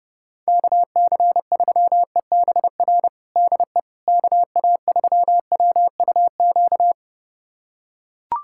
Morse code, heard as a keyed tone, about 31 wpm, 700 Hz. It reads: KC3EBR DE KA3WUQ
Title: USA Callsign DE Callsign - 6 Characters - 30wpm